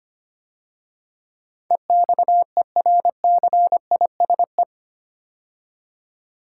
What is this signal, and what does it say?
Morse code, heard as a keyed tone, about 25 wpm, 700 Hz. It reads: EXERCISE
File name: 8 Letter Words - First 3 Letters Given - 25wpm